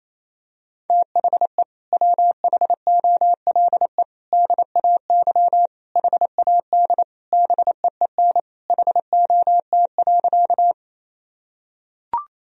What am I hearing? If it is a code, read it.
Morse code, heard as a keyed tone, about 28 wpm, 700 Hz. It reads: THE WHOLE DAY HAD BEEN HOT.
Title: Sentences from Top 1000 Words - Review Part 3 - 28wpm